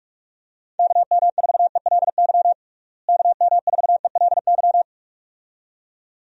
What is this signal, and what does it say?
Morse code, heard as a keyed tone, about 46 wpm, 700 Hz. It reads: KM4ELY KM4ELY
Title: USA Call Signs - 6 Characters - Repeated 2x - 45wpm